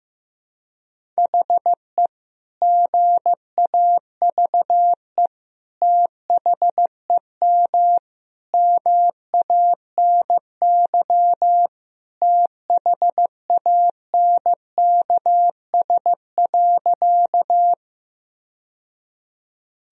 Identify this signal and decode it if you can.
Morse code, heard as a keyed tone, about 15 wpm, 700 Hz. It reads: HE GAVE THEM MANY THANKS.
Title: Sentences from Top 1000 Words - Review Part 3 - 15wpm